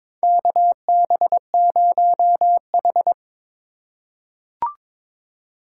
Morse code, heard as a keyed tone, about 22 wpm, 700 Hz. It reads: KB0H
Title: USA Call Signs - 4 Characters - 2x Character Spacing - 22wpm